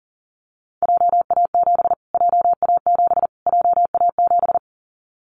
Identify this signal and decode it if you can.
Morse code, heard as a keyed tone, about 40 wpm, 700 Hz. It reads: JA7 JA7 JA7
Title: Callsign Prefixes for Top 25 DXCC - Repeated 3x - 40wpm